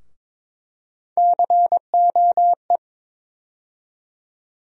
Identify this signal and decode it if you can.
Morse code, heard as a keyed tone, about 22 wpm, 700 Hz. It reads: COE
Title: USA Call Sign Postfixes - 3 Letters 22wpm